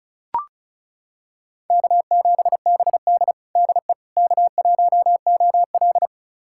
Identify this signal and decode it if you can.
Morse code, heard as a keyed tone, about 34 wpm, 700 Hz. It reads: K7BD DE K1OL
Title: USA Callsign DE Callsign - 4 Characters - 35wpm